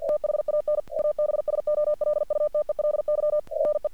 Morse code, heard as a keyed tone, about 46 wpm, 600 Hz. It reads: THAN ABSOLUTELY RE
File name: archived / CRSnet